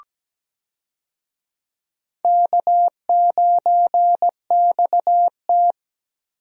Morse code, heard as a keyed tone, about 17 wpm, 700 Hz. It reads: K9XT
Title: USA Call Signs - 4 Characters - Repeated 2x - 17wpm